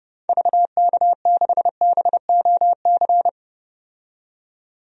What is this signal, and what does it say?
Morse code, heard as a keyed tone, about 30 wpm, 700 Hz. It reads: VK6BOC
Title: DXCC-100 Call Signs - 6 Characters - 30wpm